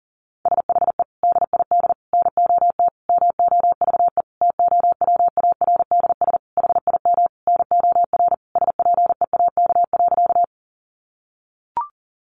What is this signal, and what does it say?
Morse code, heard as a keyed tone, about 40 wpm, 700 Hz. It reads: SHE DID NOT MOVE TOWARDS HIM NOR SPEAK.
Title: Sentences from Top 1500 Words 40wpm